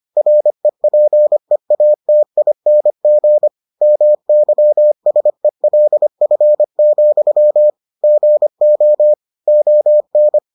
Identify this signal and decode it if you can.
Morse code, heard as a keyed tone, about 25 wpm, 600 Hz. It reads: REPEATING MYSELF, GO ON